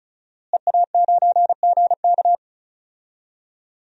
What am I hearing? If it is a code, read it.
Morse code, heard as a keyed tone, about 36 wpm, 700 Hz. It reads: EA9GK
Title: DXCC-100 Call Signs - 5 Characters - Repeated 2x - 35wpm